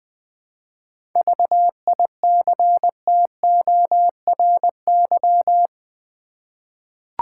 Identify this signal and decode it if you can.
Morse code, heard as a keyed tone, about 20 wpm, 700 Hz. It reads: VICTORY
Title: 7 Letter Words - First 3 Letters Given - 20wpm